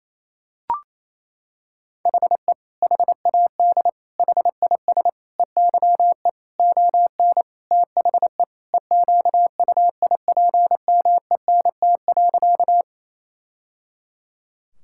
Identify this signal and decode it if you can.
Morse code, heard as a keyed tone, about 28 wpm, 700 Hz. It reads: HE HAD HIS EYE ON THE EQUIPMENT.
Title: Sentences from Top 1000 Words - Review Part 5 - 28wpm